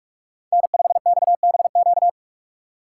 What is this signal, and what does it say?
Morse code, heard as a keyed tone, about 44 wpm, 700 Hz. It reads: N5XBQ